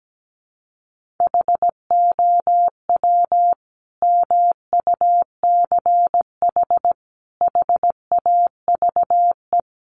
Morse code, heard as a keyed tone, about 17 wpm, 700 Hz. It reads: HOW MUCH HAVE
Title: Sentences from Top 300 Words 17wpm